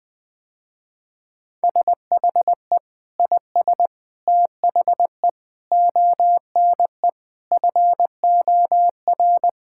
Morse code, heard as a keyed tone, about 20 wpm, 700 Hz. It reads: SHE IS THE ONE FOR
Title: Sentences with 2-Word Phrases - Spoken Beginning - 20wpm